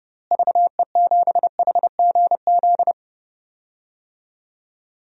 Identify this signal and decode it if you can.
Morse code, heard as a keyed tone, about 30 wpm, 700 Hz. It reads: VE7HGZ
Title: DXCC-100 Call Signs - 6 Characters - Repeated 2x - 30wpm